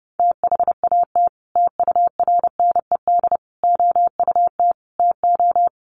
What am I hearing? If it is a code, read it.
Morse code, heard as a keyed tone, about 30 wpm, 700 Hz. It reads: THAT TURNED OUT TO
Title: Sets of 5 Words Encore2 30wpm